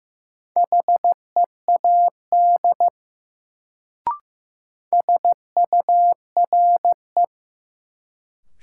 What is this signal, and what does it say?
Morse code, heard as a keyed tone, about 15 wpm, 700 Hz. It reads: HEAD SURE
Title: Sets of 3 Words - New Word Drill - 15wpm